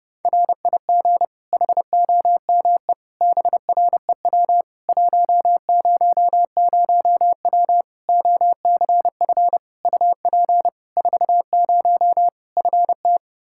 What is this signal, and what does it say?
Morse code, heard as a keyed tone, about 30 wpm, 700 Hz. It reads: RIG HOME BREW 100W OCF UP 40 FT
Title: K1USN SST QSO Practice 30wpm